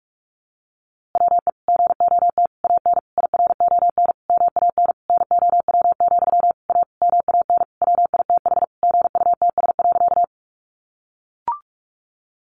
Morse code, heard as a keyed tone, about 45 wpm, 700 Hz. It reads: WE GOT AN IRON MAN NOW, A MAN WITH GUTS.
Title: A Sound Of Thunder 45wpm